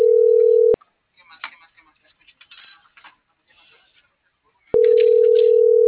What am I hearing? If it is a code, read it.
Ringback tone (North America), 440 + 480 Hz.
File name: CENTRAL MORENA / out jacque.aguilera.25 - .9013